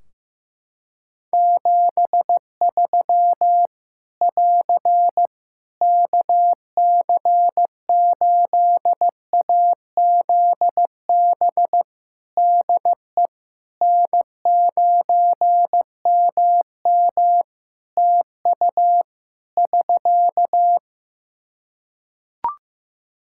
Morse code, heard as a keyed tone, about 15 wpm, 700 Hz. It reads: 73 <AR> KC8AZB DE N9MM TU <SK>
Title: K1USN SST QSO Practice 15wpm